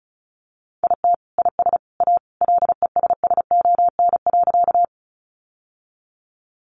Morse code, heard as a keyed tone, about 34 wpm, 700 Hz. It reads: IT IS A LESSON.
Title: Sentences from Top 1200 Words 35wpm